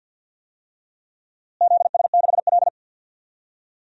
Morse code, heard as a keyed tone, about 50 wpm, 700 Hz. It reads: ZS6L
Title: DXCC-100 Call Signs - 4 Characters - Repeated 2x - 50wpm